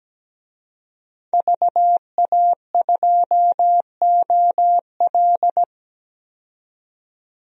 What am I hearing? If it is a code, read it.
Morse code, heard as a keyed tone, about 17 wpm, 700 Hz. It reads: VA2OL
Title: DXCC-100 Call Signs - 5 Characters - 17wpm